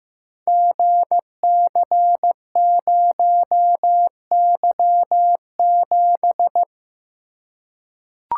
Morse code, heard as a keyed tone, about 15 wpm, 700 Hz. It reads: GC0Y7
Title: Code Groups - 5 Letter:Numbers - 15wpm